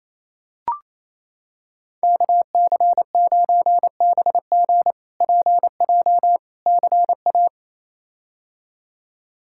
Morse code, heard as a keyed tone, about 28 wpm, 700 Hz. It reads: KC9BG PJ CA
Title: K1USN SST QSO Practice - 28wpm